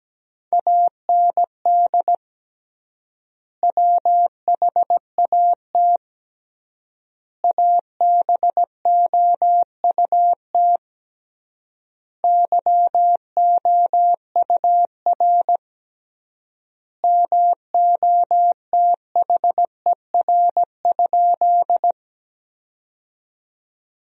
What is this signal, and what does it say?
Morse code, heard as a keyed tone, about 17 wpm, 700 Hz. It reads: AND WHAT ABOUT YOUR MOTHER?
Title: 3x Word-Spacing - Sentences from Top 200 Words - 17wpm